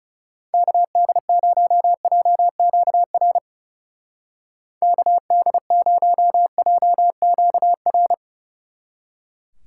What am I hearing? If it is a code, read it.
Morse code, heard as a keyed tone, about 33 wpm, 700 Hz. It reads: KD0JQR KD0JQR
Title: Speed Racing - USA 2x3 Call Signs 30wpm